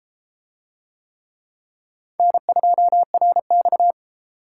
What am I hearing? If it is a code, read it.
Morse code, heard as a keyed tone, about 30 wpm, 700 Hz. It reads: N2RX